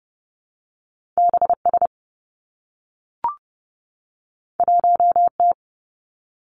Morse code, heard as a keyed tone, about 30 wpm, 700 Hz. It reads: B S 1 T
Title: Two Challenging Letter-Number Combinations 30wpm